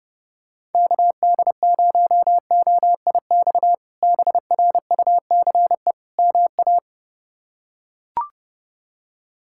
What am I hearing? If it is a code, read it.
Morse code, heard as a keyed tone, about 32 wpm, 700 Hz. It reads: KD0OIX BRUCE MA
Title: K1USN SST QSO Practice - 30wpm